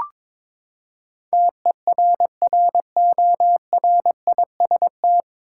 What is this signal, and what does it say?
Morse code, heard as a keyed tone, about 22 wpm, 700 Hz. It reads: TERRORIST
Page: Morse Code Ninja practice 22wpm